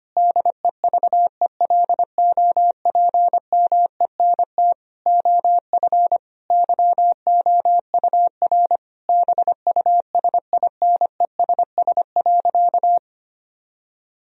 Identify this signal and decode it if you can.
Morse code, heard as a keyed tone, about 25 wpm, 700 Hz. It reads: DEVELOPMENT OF YOUR BUSINESS.
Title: Sentences from Top 1000 Words - Review Part 4 - 25wpm